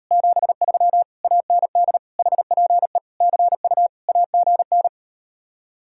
Morse code, heard as a keyed tone, about 39 wpm, 700 Hz. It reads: 73 AND HPE CU AGN